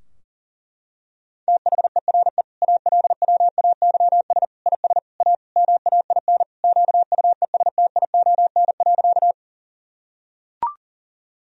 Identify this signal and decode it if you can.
Morse code, heard as a keyed tone, about 40 wpm, 700 Hz. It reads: THERE ALWAYS IS A MAIN QUESTION.
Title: Sentences from Top 1000 Words - Review Part 2 - 40wpm